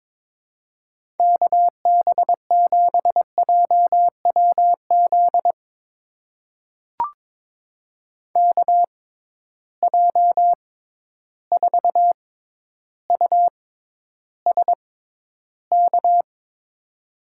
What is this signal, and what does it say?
Morse code, heard as a keyed tone, about 22 wpm, spaced as for 11 wpm, 700 Hz. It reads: KB7JWZ KJ4USK
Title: USA Call Signs - 6 Characters - 6x Character Spacing - 22wpm